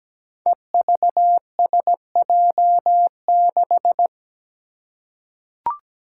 Morse code, heard as a keyed tone, about 17 wpm, 700 Hz. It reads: EVSJ6
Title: Five Challenging Letter-Number Combinations 17wpm